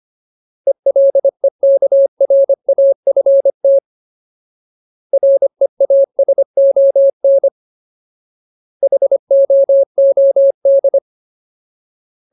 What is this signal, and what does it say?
Morse code, heard as a keyed tone, about 25 wpm, 550 Hz. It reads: ELEKRAFT REASON HOOD
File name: Ultimate Word List / 25WPM Ultimate